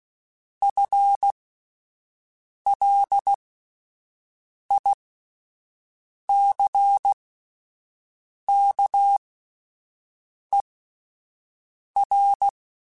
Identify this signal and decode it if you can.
Morse code, heard as a keyed tone, about 16 wpm, spaced as for 5 wpm, 800 Hz. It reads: FLICKER
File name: Spaced / FL Runs sp mp3s / FL Runs sp @10